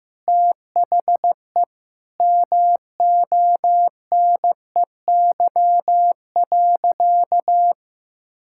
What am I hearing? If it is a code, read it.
Morse code, heard as a keyed tone, about 15 wpm, 700 Hz. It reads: THE MONEY.
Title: Sentences from Top 1000 Words - Review Part 5 - 15wpm